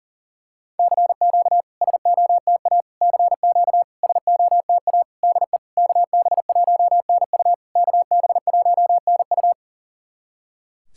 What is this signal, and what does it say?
Morse code, heard as a keyed tone, about 44 wpm, 700 Hz. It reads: CQ SOTA CQ SOTA DE KB1NU KB1NU
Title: SOTA QSO Practice 40wpm